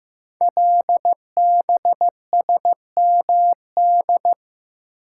Morse code, heard as a keyed tone, about 15 wpm, 700 Hz. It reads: LBSMD
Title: Code Groups - 5 Letters - 15wpm